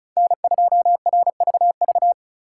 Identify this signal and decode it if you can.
Morse code, heard as a keyed tone, about 43 wpm, 700 Hz. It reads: N2RVV
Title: Speed Racing - USA 1x3 Call Signs 35wpm